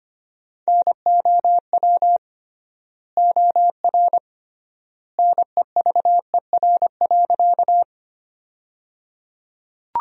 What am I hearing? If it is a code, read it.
Morse code, heard as a keyed tone, about 25 wpm, 700 Hz. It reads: NOW OR NEVER.
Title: 3x Word-Spacing - Sentences from Top 300 Words - 25wpm